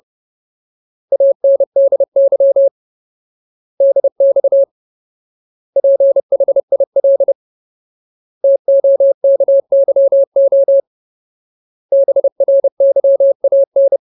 Morse code, heard as a keyed tone, about 30 wpm, 550 Hz. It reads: ANDY DX PHIL TOKYO BRYAN
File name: Ultimate Word List / 30WPM Ultimate